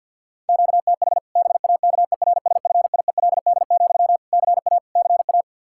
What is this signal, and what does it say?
Morse code, heard as a keyed tone, about 52 wpm, 700 Hz. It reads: QTH BAKERSFIELD, CA CA
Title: Practice QSO - Exchange 1 - Beginning - 50wpm